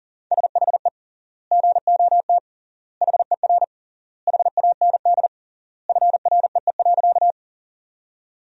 Morse code, heard as a keyed tone, about 40 wpm, 700 Hz. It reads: SHE GOT HER HAND FREE.
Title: 3x Word-Spacing - Sentences from Top 500 Words - 40wpm